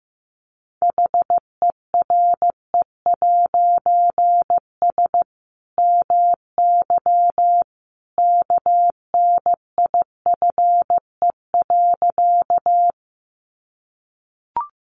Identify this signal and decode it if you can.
Morse code, heard as a keyed tone, about 15 wpm, 700 Hz. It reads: HERE'S MY KNIFE.
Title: A Sound Of Thunder 15wpm